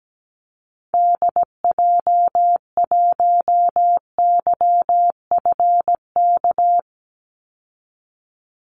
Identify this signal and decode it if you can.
Morse code, heard as a keyed tone, about 17 wpm, 700 Hz. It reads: DJ1YFK
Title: Call Signs Repeated 2x - 17wpm